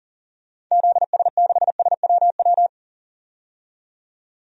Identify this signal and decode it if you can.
Morse code, heard as a keyed tone, about 40 wpm, 700 Hz. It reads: ZS6SWW
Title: DXCC-100 Call Signs - 6 Characters - 40wpm